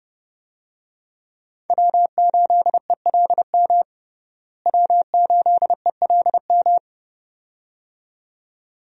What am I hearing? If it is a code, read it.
Morse code, heard as a keyed tone, about 32 wpm, 700 Hz. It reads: W8ELM W8ELM
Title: USA Call Signs - 5 Characters - Repeated 2x - 30wpm